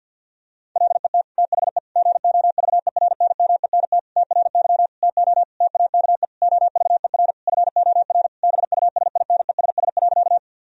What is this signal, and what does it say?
Morse code, heard as a keyed tone, about 50 wpm, 700 Hz. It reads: LET THE GOVERNMENT TRY TO TAKE OVER FOR BUSINESS.
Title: Sentences from Top 1000 Words - Review Part 3 - 50wpm